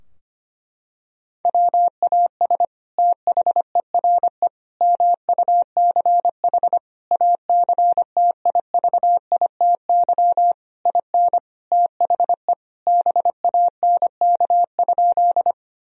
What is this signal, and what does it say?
Morse code, heard as a keyed tone, about 25 wpm, 700 Hz. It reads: WAS THERE MUCH ACTIVITY IN THE BANK?
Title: Sentences from Top 1000 Words - Review Part 4 - 25wpm